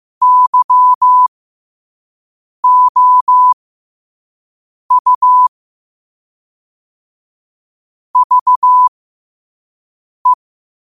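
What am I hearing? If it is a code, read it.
Morse code, heard as a keyed tone, about 15 wpm, spaced as for 5 wpm, 1000 Hz. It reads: YOU VE
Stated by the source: Morse code quotes sent at 5 words per minute.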